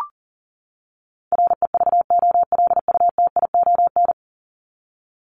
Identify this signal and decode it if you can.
Morse code, heard as a keyed tone, about 40 wpm, 700 Hz. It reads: REVOLUTION
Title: Top 1751-2000 Words 40wpm